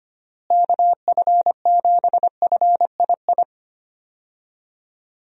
Morse code, heard as a keyed tone, about 25 wpm, 700 Hz. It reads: KF7FII
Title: USA Call Signs - 6 Characters - Repeated 2x - 25wpm